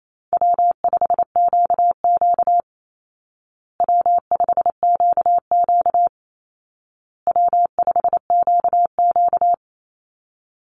Morse code, heard as a keyed tone, about 28 wpm, 700 Hz. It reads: W5QQ W5QQ W5QQ
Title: Easy Call Signs Repeated 3x - 28wpm